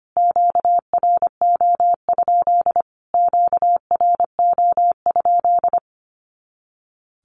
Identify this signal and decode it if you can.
Morse code, heard as a keyed tone, about 25 wpm, 700 Hz. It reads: QRO? QRO?
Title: Q-Codes Repeated 3x 25wpm